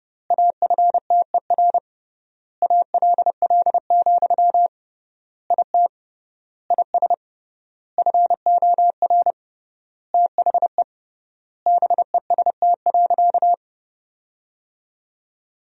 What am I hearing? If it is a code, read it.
Morse code, heard as a keyed tone, about 32 wpm, 700 Hz. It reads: AFTER ALL, IT IS FOR THE BEST.
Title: 3x Word-Spacing - Sentences from Top 500 Words - 30wpm